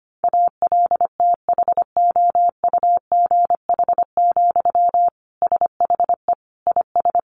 Morse code, heard as a keyed tone, about 25 wpm, 700 Hz. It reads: ALTHOUGH, SHE IS
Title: Sentences from Top 600 Words 25wpm